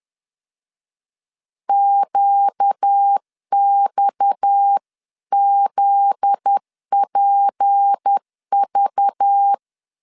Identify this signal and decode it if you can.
Morse code, heard as a keyed tone, about 10 wpm, 800 Hz. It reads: QXZPV
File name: LEZIONI / 7 GF9Z / 7 GF9Z 10wpm 2611